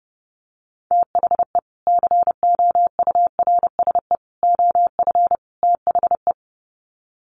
Morse code, heard as a keyed tone, about 30 wpm, 700 Hz. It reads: THE COURSE OF THE
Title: Sets of 4 Words 30wpm